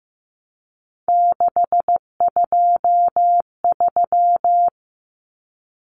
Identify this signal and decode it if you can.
Morse code, heard as a keyed tone, about 15 wpm, 700 Hz. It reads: 623
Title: Common Cut Numbers - 3-Digit 15wpm